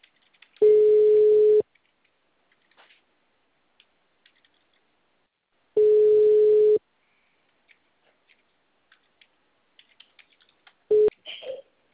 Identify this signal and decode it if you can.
Ringback tone (Europe, 425 Hz).
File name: out- -2012- - - .13519